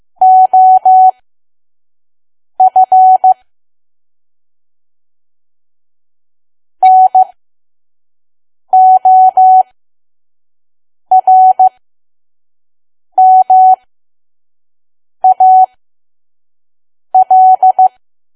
Morse code, heard as a keyed tone, about 15 wpm, spaced as for 5 wpm, 750 Hz. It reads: OF NORMAL
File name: Archive / 5 WPM files / PM